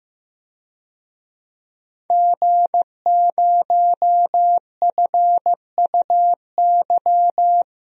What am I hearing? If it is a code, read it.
Morse code, heard as a keyed tone, about 15 wpm, 700 Hz. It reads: G0FUY
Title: DXCC-100 Call Signs - 5 Characters - Repeated 2x - 15wpm